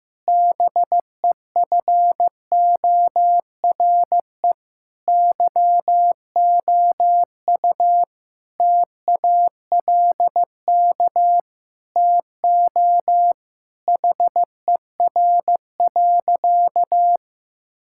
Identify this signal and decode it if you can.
Morse code, heard as a keyed tone, about 15 wpm, 700 Hz. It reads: BEFORE YOU TALK TO HER.
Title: Sentences from Top 1000 Words - Review Part 6 - 15wpm